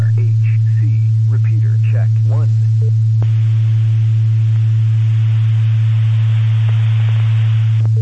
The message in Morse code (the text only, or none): none